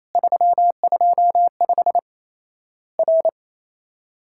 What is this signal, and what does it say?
Morse code, heard as a keyed tone, about 28 wpm, 700 Hz. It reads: 325 R
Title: Worked All Europe DX Contest - QTC Exchange - 28wpm